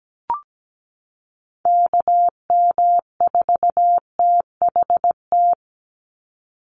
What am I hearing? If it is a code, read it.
Morse code, heard as a keyed tone, about 17 wpm, 700 Hz. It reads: KM4THT
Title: USA Call Signs 17wpm